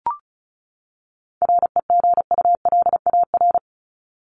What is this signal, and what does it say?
Morse code, heard as a keyed tone, about 34 wpm, 700 Hz. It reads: REGULAR
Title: Top 751-1000 Words 35wpm